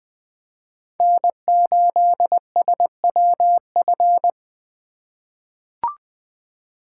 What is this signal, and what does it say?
Morse code, heard as a keyed tone, about 20 wpm, 700 Hz. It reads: N8SWF
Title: USA Call Signs - 4x Character Spacing - 20wpm